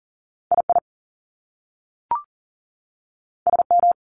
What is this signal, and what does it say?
Morse code, heard as a keyed tone, about 40 wpm, 700 Hz. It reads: I I S M
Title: Top Two Letter Combinations in Words 40wpm